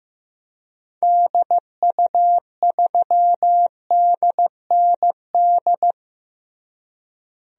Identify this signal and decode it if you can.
Morse code, heard as a keyed tone, about 15 wpm, 700 Hz. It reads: DU3DND
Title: DXCC-100 Call Signs - 6 Characters - Repeated 2x - 15wpm